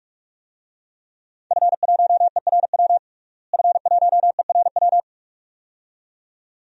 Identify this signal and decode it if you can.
Morse code, heard as a keyed tone, about 46 wpm, 700 Hz. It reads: F1ERW F1ERW
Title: DXCC-100 Call Signs - 5 Characters - Repeated 2x - 45wpm